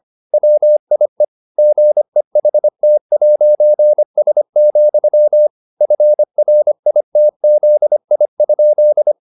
Morse code, heard as a keyed tone, about 25 wpm, 600 Hz. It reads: WIE GEHT'S, FRITZI?